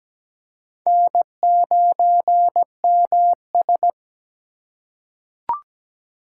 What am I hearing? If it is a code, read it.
Morse code, heard as a keyed tone, about 17 wpm, 700 Hz. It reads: N9MS
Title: K1USN SST QSO Practice - 17wpm